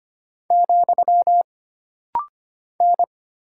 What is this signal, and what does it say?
Morse code, heard as a keyed tone, about 25 wpm, 700 Hz. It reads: , N
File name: Single Character (Letter:Number:Punctuation:Prosign) - Rapid-Fire - 25wpm